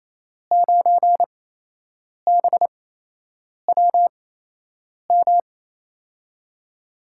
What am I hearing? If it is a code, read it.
Morse code, heard as a keyed tone, about 28 wpm, spaced as for 8 wpm, 700 Hz. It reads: 9BWM
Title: USA Call Signs - 8x Character Spacing - 28wpm